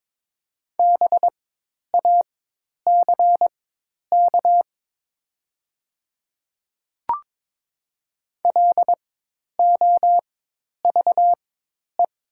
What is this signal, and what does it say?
Morse code, heard as a keyed tone, about 22 wpm, spaced as for 10 wpm, 700 Hz. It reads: BACK LOVE
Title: 4x Character Spacing - Top 500 Words - 22wpm